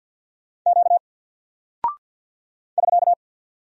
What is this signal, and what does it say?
Morse code, heard as a keyed tone, about 49 wpm, 700 Hz. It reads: Q <SK>
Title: Single Character (Letter:Number:Punctuation:Prosign) 50wpm